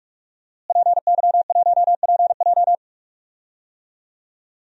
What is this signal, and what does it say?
Morse code, heard as a keyed tone, about 42 wpm, 700 Hz. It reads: PY1PJ
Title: DXCC-100 Call Signs - 5 Characters - Repeated 3x - 45wpm